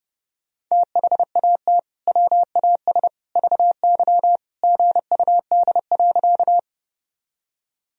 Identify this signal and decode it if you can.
Morse code, heard as a keyed tone, about 30 wpm, 700 Hz. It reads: THAT WAS VY GUD.
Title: Sentences from Top 200 Words and Abbreviations 30wpm